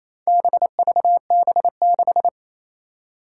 Morse code, heard as a keyed tone, about 28 wpm, 700 Hz. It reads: BVB6
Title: Four Challenging Letter-Number Combinations 28wpm